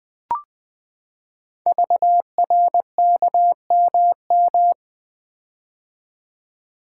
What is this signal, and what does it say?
Morse code, heard as a keyed tone, about 20 wpm, 700 Hz. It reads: VRKMM
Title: Code Groups - 5 Letters - 20wpm